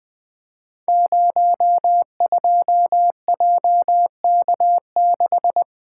Morse code, heard as a keyed tone, about 21 wpm, 700 Hz. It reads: 02JK6
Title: Code Groups - 5 Letter:Numbers - 20wpm